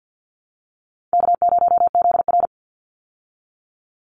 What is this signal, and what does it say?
Morse code, heard as a keyed tone, about 50 wpm, 700 Hz. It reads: K0ZR
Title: Easy Call Signs Repeated 3x - 50wpm